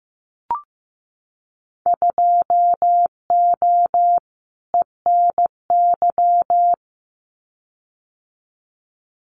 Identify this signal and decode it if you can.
Morse code, heard as a keyed tone, about 15 wpm, 700 Hz. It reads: 2O ENY
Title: Winter Field Day Exchanges Encore 15wpm